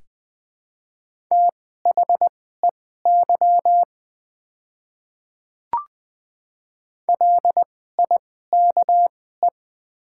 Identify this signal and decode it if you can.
Morse code, heard as a keyed tone, about 20 wpm, spaced as for 14 wpm, 700 Hz. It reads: THEY LIKE
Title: 2x Character Spacing - Top 200 Words - 20wpm